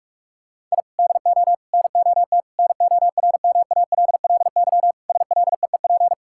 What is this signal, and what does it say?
Morse code, heard as a keyed tone, about 46 wpm, 700 Hz. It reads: I DO NOT NORMALLY SLEEP
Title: Sentences from Top 1000 Words - Review Part 6 - 45wpm